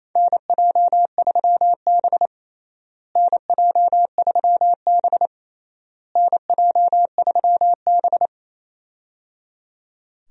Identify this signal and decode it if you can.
Morse code, heard as a keyed tone, about 28 wpm, 700 Hz. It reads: NJ3B NJ3B NJ3B
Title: USA Call Signs - 4 Characters - Repeated 3x - 28wpm